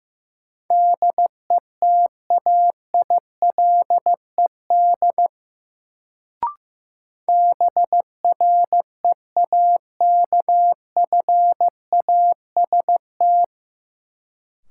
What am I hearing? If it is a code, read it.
Morse code, heard as a keyed tone, about 16 wpm, 700 Hz. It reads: DETAILED BREAKFAST
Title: Sentences from Top 1500 Words - New Word Drill - 15wpm